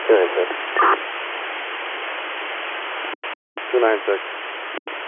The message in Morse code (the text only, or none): none